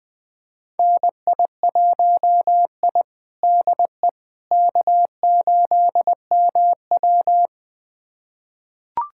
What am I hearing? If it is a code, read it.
Morse code, heard as a keyed tone, about 20 wpm, 700 Hz. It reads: NI1I DE K8MW
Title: USA Callsign DE Callsign - 4 Characters - 20wpm